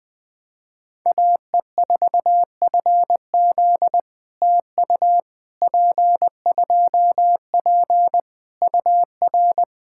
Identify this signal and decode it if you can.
Morse code, heard as a keyed tone, about 20 wpm, 700 Hz. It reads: AE4FZ TU P2P UR
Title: POTA QSO Practice - 20wpm